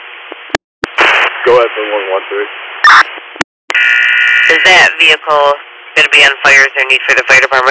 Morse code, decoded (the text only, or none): none